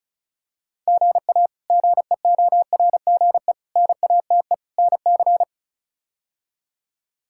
Morse code, heard as a keyed tone, about 34 wpm, 700 Hz. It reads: GA GEORGE NATE NC
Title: K1USN SST QSO Practice - 35wpm